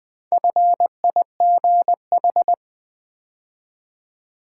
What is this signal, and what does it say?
Morse code, heard as a keyed tone, about 20 wpm, 700 Hz. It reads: FIGH
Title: Build A Word - 2 Letters At A Time - 8-Letter Words - 20wpm